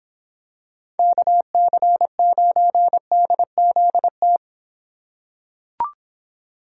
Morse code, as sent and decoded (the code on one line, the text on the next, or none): -.- -.-. ----. -.. --.. -
KC9DZT